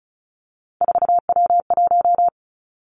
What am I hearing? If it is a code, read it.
Morse code, heard as a keyed tone, about 36 wpm, 700 Hz. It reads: 4W1
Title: Three Challenging Letter-Number Combinations 35wpm